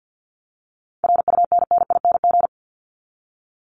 Morse code, heard as a keyed tone, about 48 wpm, 700 Hz. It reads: RUNNING
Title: ING Suffix - Top 100 Words with Base Words 50wpm